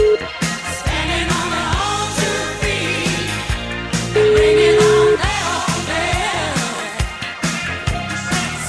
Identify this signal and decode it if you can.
Ringback tone (Europe, 425 Hz), with noise.